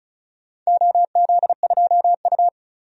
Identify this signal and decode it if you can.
Morse code, heard as a keyed tone, about 36 wpm, 700 Hz. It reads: OZ2U
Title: DXCC-100 Call Signs - 4 Characters - 35wpm